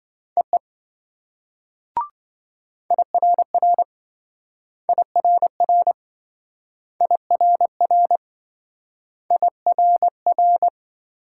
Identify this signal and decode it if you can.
Morse code, heard as a keyed tone, about 26 wpm, 700 Hz. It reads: EE IRR IRR IRR IRR